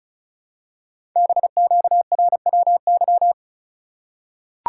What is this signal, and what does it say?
Morse code, heard as a keyed tone, about 36 wpm, 700 Hz. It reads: BQRWY
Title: Code Groups - 5 Letters - 35wpm